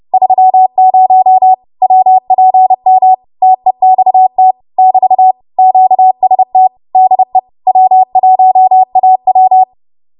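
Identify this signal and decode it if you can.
Morse code, heard as a keyed tone, about 30 wpm, 750 Hz. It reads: 30 WPM TEXT <BT> QST DE W1AW
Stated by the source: W1AW Web Code Practice Files, 30 WPM run from ARRL; text from April 2023 QST.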